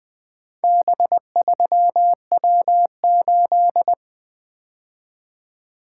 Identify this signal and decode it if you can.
Morse code, heard as a keyed tone, about 20 wpm, 700 Hz. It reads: B3W8
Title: Four Letter-Number Combinations - 20wpm